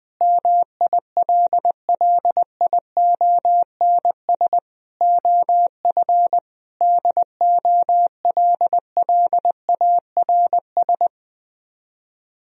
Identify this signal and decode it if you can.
Morse code, heard as a keyed tone, about 21 wpm, 700 Hz. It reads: MILLIONS OF DOLLARS
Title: Speed-Racing - Sets of 5 Words 20wpm